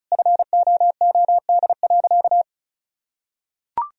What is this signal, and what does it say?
Morse code, heard as a keyed tone, about 34 wpm, 700 Hz. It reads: FOOD.
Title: Sentences from Top 1000 Words - Review Part 4 - 35wpm